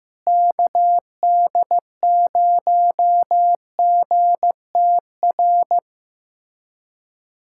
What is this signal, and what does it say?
Morse code, heard as a keyed tone, about 16 wpm, 700 Hz. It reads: KD0GTR
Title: USA Callsign DE Callsign - 6 Characters - 15wpm